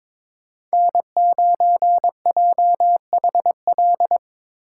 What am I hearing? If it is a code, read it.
Morse code, heard as a keyed tone, about 22 wpm, 700 Hz. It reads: N9JHL
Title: K1USN SST QSO Practice - 22wpm